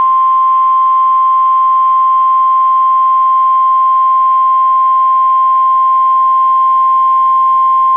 Emergency Alert System (SAME) tones (the 1050 Hz attention tone). It is heard from 0.2 to 7.5 s.